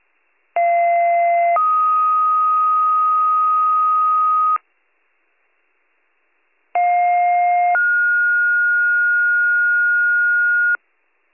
Two-tone page, heard as a one-second tone, then a three-second tone, about 690 then 1250 Hz, about 710 then 1430 Hz. It is heard from 0.6 to 10.8 s.